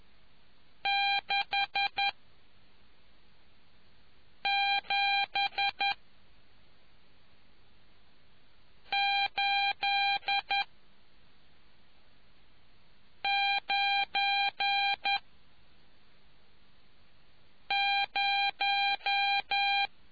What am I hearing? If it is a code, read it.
Morse code, heard as a keyed tone, about 11 wpm, spaced as for 3 wpm, 800 Hz. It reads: 67890